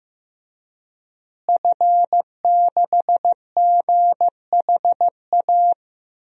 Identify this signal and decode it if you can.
Morse code, heard as a keyed tone, about 15 wpm, 700 Hz. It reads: F6GHA
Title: DXCC-100 Call Signs - 5 Characters - Repeated 2x - 15wpm